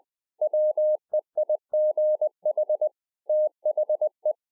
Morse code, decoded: WEIGH THE

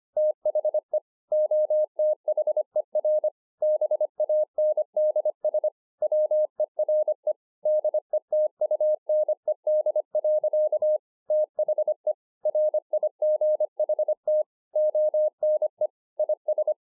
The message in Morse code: THE OTHER BANDS WERE DETUNED. THE RIGHT ONE IS

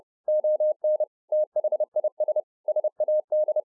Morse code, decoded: ON THIS SAD

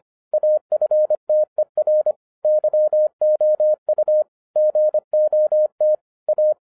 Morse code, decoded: AFTER YOU GOT A

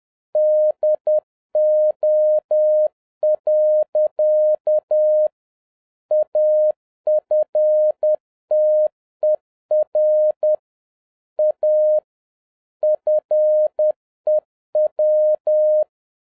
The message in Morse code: DO. AFTER A FEW

